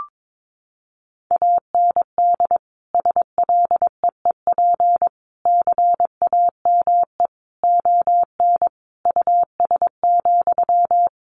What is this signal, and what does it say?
Morse code, heard as a keyed tone, about 22 wpm, 700 Hz. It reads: AND SLEEP CAME ON US,
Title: Sentences from Top 1500 Words 22wpm